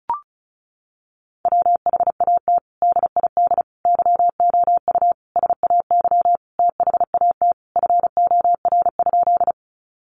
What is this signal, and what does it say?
Morse code, heard as a keyed tone, about 34 wpm, 700 Hz. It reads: WHAT DID YOU SAY THAT FOR?
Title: Sentences from Top 200 Words 35wpm